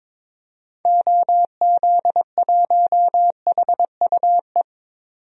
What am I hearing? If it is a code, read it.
Morse code, heard as a keyed tone, about 22 wpm, 700 Hz. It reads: OZ1HUE
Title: DXCC-100 Call Signs - 6 Characters - 22wpm